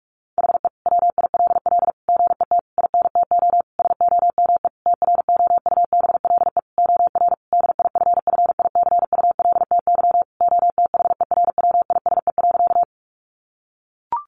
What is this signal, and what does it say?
Morse code, heard as a keyed tone, about 44 wpm, 700 Hz. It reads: HE WILL GET INTO SOME TROUBLE OR DIFFICULTY OTHERWISE.
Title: Sentences from Top 1500 Words 45wpm